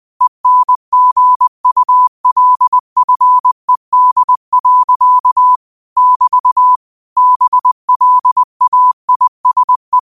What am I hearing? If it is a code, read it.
Morse code, heard as a keyed tone, about 20 wpm, 1000 Hz. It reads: ENGULFED. <BT> BLAISE